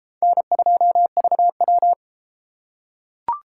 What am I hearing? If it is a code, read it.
Morse code, heard as a keyed tone, about 32 wpm, 700 Hz. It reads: N2VW